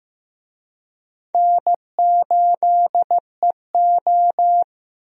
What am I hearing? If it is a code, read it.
Morse code, heard as a keyed tone, about 15 wpm, 700 Hz. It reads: N8EO
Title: USA Call Signs - 4 Characters - 15wpm